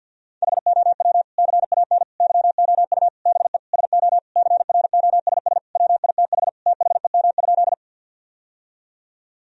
Morse code, heard as a keyed tone, about 50 wpm, 700 Hz. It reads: HOW CAN YOU BE SO CROSS WITH THEM?
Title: Sentences from Top 1000 Words - Review Part 1 - 50wpm